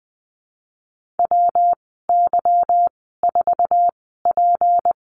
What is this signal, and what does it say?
Morse code, heard as a keyed tone, about 20 wpm, spaced as for 15 wpm, 700 Hz. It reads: WY4P
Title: 161 Lesson 26g - Letter g - Call Signs 20wpm